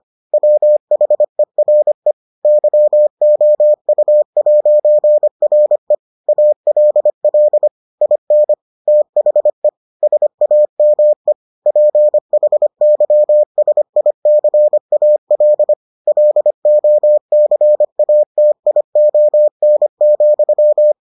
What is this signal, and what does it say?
Morse code, heard as a keyed tone, about 25 wpm, 600 Hz. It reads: WHERE YOU'RE ALL IN THE SAME PHYSICAL LOCATION,